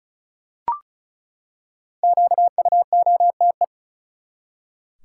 Morse code, heard as a keyed tone, about 35 wpm, 700 Hz. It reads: QUOTE
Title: Five Letter Words - 35wpm